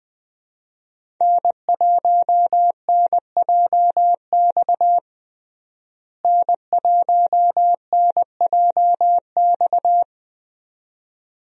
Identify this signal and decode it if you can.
Morse code, heard as a keyed tone, about 20 wpm, 700 Hz. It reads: N1NJX N1NJX
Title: USA Call Signs - 5 Characters - Repeated 2x - 20wpm